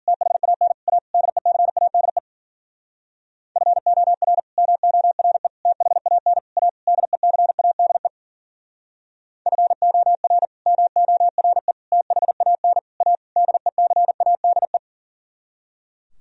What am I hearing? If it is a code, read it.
Morse code, heard as a keyed tone, about 52 wpm, 700 Hz. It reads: THAN A DECADE FOR MORE THAN A DECADE FOR MORE THAN A DECADE